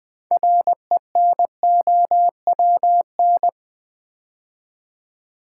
Morse code, heard as a keyed tone, about 20 wpm, 700 Hz. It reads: RENOWN
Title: Build A Word - 3 Letters At A Time - 8-Letter Words - 20wpm